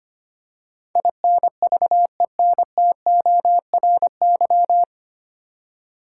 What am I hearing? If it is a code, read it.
Morse code, heard as a keyed tone, about 25 wpm, 690 Hz. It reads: INVENTORY
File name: practice / 25wpm / 9 Letter Words - First 3 Letters Given - 25wpm